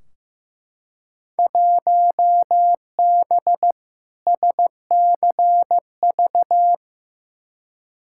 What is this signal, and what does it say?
Morse code, heard as a keyed tone, about 15 wpm, 700 Hz. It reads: 1B SCV
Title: Field Day Exchanges - 15wpm